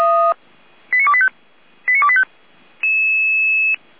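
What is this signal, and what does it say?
Five-tone selective call (ZVEI), heard as quick runs of five tones, tones of about 70 ms. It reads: 89285 89285
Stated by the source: ZVEI five-tone call received by SDR.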